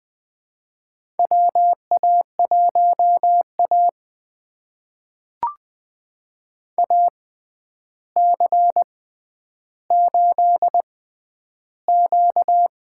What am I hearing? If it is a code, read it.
Morse code, heard as a keyed tone, about 20 wpm, spaced as for 10 wpm, 700 Hz. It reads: WA1A AC8Q
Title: USA Call Signs - 4 Characters - 6x Character Spacing - 20wpm